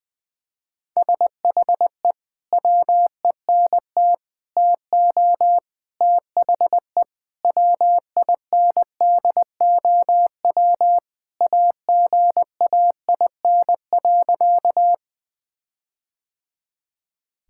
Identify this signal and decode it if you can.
Morse code, heard as a keyed tone, about 21 wpm, 700 Hz. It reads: SHE WENT TO THE WINDOW AGAIN.
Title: Sentences from Top 1600 Words 20wpm